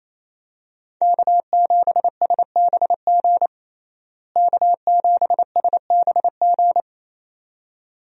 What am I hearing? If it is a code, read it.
Morse code, heard as a keyed tone, about 28 wpm, 700 Hz. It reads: K7SBG K7SBG
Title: USA Call Signs - 5 Characters - Repeated 2x - 28wpm